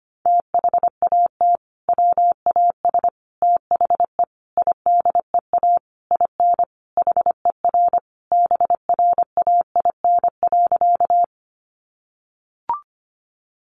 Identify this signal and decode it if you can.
Morse code, heard as a keyed tone, about 25 wpm, 700 Hz. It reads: THAT WAS THE IDEA IN HER BRAIN.
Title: Sentences from Top 1100 Words 25wpm